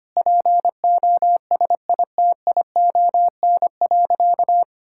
Morse code, heard as a keyed tone, about 25 wpm, 700 Hz. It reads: POSITION.
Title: Sentences from Top 1000 Words - Review Part 5 - 25wpm